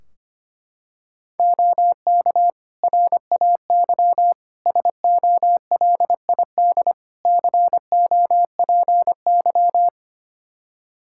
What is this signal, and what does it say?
Morse code, heard as a keyed tone, about 25 wpm, 700 Hz. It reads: OK RAY SOLID COPY
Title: Practice QSO - Exchange 3 - More Info - 25wpm